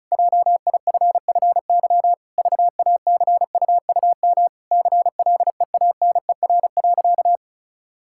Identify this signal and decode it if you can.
Morse code, heard as a keyed tone, about 36 wpm, 700 Hz. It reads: JIFFY VACUUM CLEANER.
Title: Pangram Challenge v2 - 35wpm